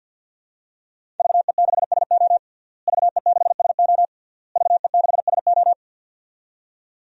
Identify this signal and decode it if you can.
Morse code, heard as a keyed tone, about 45 wpm, 700 Hz. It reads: VE6SO VE6SO VE6SO
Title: DXCC-100 Call Signs - 5 Characters - Repeated 3x - 50wpm